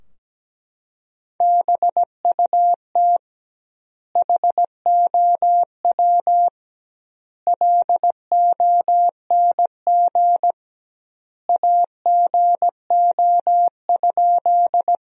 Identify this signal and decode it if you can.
Morse code, heard as a keyed tone, about 17 wpm, 700 Hz. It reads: BUT HOW LONG AGO?
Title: 2x Word-Spacing - Sentences from Top 500 Words - 17wpm